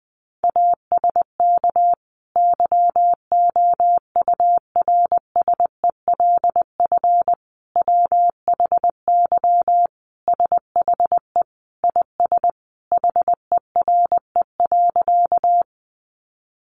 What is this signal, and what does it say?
Morse code, heard as a keyed tone, about 20 wpm, 700 Hz. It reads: ASK YOURSELF WHY SHE IS HERE.
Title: Sentences from Top 600 Words 20wpm